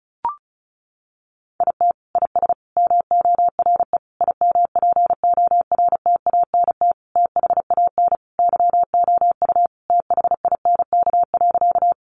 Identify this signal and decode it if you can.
Morse code, heard as a keyed tone, about 34 wpm, 700 Hz. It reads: IT IS MORE IMPORTANT THAN YOU THINK.
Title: Sentences from Top 600 Words 35wpm